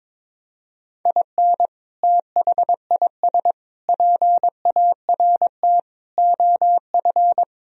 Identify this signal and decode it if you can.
Morse code, heard as a keyed tone, about 22 wpm, 700 Hz. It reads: IN THIS PART OF
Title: Speed-Racing - Sets of 5 Words 22wpm